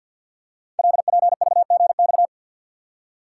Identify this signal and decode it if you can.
Morse code, heard as a keyed tone, about 48 wpm, 700 Hz. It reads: LPVGX
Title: Code Groups - 5 Letters - 50wpm